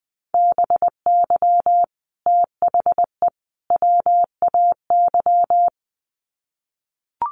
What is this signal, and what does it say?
Morse code, heard as a keyed tone, about 20 wpm, 700 Hz. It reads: BY THE WAY
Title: Sets of 3 Words 20wpm